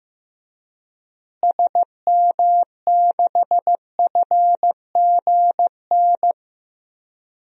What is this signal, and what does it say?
Morse code, heard as a keyed tone, about 15 wpm, 700 Hz. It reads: SM6FGN
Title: DXCC-100 Call Signs - 6 Characters - Repeated 3x - 15wpm